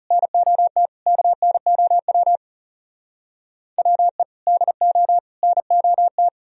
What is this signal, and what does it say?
Morse code, heard as a keyed tone, about 40 wpm, 700 Hz. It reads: NOT KNOW WE DO NOT